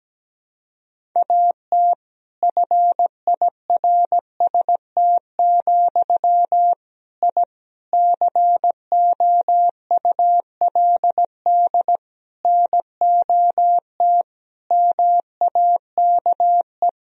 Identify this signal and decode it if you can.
Morse code, heard as a keyed tone, about 17 wpm, 700 Hz. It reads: AT FIRST, I COULD NOT MAKE
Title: Sentences from Top 1000 Words - Review Part 4 - 17wpm